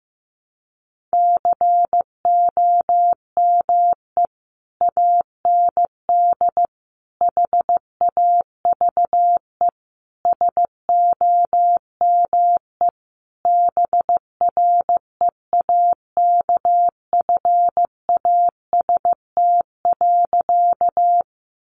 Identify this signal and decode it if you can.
Morse code, heard as a keyed tone, about 15 wpm, 700 Hz. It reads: COME AND HAVE SOME BREAKFAST.
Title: Sentences from Top 1500 Words 15wpm